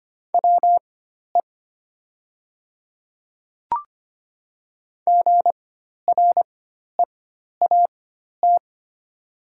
Morse code, heard as a keyed tone, about 25 wpm, spaced as for 12 wpm, 700 Hz. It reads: WE GREAT
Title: 4x Character Spacing - Top 200 Words - 25wpm